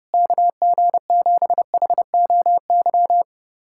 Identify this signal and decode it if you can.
Morse code, heard as a keyed tone, about 30 wpm, 700 Hz. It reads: KG7HOY
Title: USA Callsign DE Callsign 30wpm